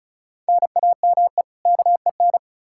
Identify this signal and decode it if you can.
Morse code, heard as a keyed tone, about 36 wpm, 700 Hz. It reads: NAME KEN
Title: K1USN SST QSO Practice 35wpm